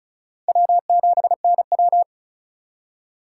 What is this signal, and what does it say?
Morse code, heard as a keyed tone, about 42 wpm, 700 Hz. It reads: W7NW
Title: Speed Racing - USA 1x2 Call Signs 35wpm